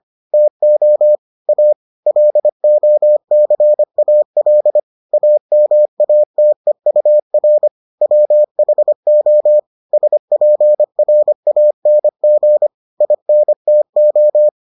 Morse code, heard as a keyed tone, about 25 wpm, 600 Hz. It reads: TO A LOCAL AMATEUR WHO SPRANG INTO